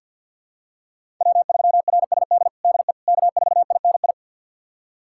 Morse code, heard as a keyed tone, about 51 wpm, 700 Hz. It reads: W3RSD DE K4INI
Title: USA Callsign DE Callsign - 5 Characters - 50wpm